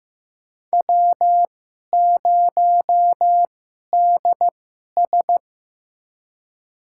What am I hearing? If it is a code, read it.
Morse code, heard as a keyed tone, about 15 wpm, spaced as for 11 wpm, 700 Hz. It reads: W0DS
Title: USA Call Signs - 2x Character Spacing - 15wpm